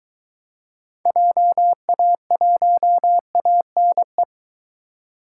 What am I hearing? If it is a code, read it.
Morse code, heard as a keyed tone, about 23 wpm, 700 Hz. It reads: JA1ANE